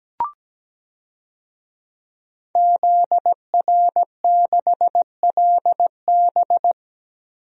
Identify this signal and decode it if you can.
Morse code, heard as a keyed tone, about 17 wpm, 700 Hz. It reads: ZR6LB
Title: DXCC-100 Call Signs - 5 Characters - 17wpm